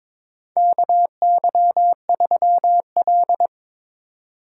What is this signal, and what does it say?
Morse code, heard as a keyed tone, about 22 wpm, 700 Hz. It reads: KY3L
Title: USA Call Signs - 4 Characters - 3x Character Spacing - 22wpm